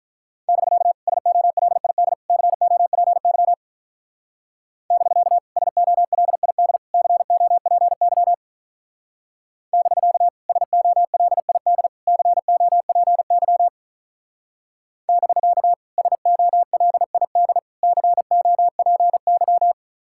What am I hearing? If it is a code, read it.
Morse code, heard as a keyed tone, about 43 wpm, 700 Hz. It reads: <BK> SOLID COPY <BK> SOLID COPY <BK> SOLID COPY <BK> SOLID COPY
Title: Speed-Racing Common QSO Phrases - 35wpm